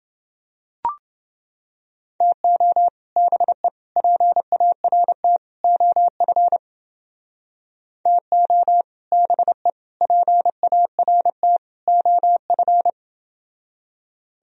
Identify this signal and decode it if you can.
Morse code, heard as a keyed tone, about 29 wpm, 700 Hz. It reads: TO BE PART OF TO BE PART OF